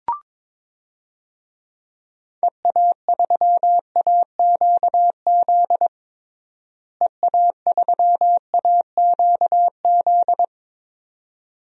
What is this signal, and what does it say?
Morse code, heard as a keyed tone, about 22 wpm, 700 Hz. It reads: EA3AQZ EA3AQZ
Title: DXCC-100 Call Signs - 6 Characters - Repeated 2x - 22wpm